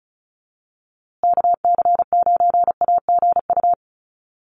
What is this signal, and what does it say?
Morse code, heard as a keyed tone, about 34 wpm, 700 Hz. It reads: KC9AGU
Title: USA Call Signs Repeated 2x 35wpm